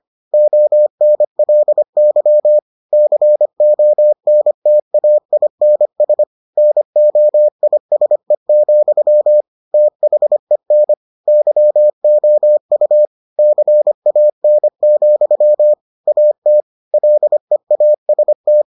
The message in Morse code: ONLY CONTAINS NOISE, THEN YOU CAN, AT LEAST

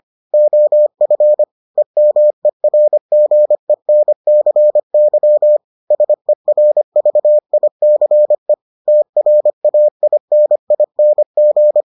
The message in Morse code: OF EMERGENCY SERVICE TRAINING

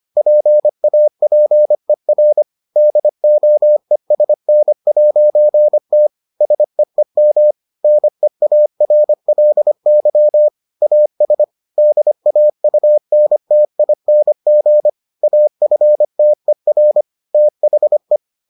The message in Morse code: PAPER DOESN'T SEEM NEARLY AS DAUNTING AFTER THE

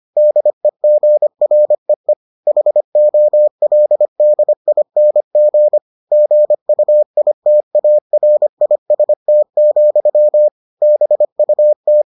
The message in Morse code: DEGREE HOLDING GUITARIST, BUT

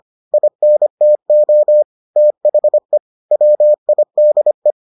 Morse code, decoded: INTO THE WIDE

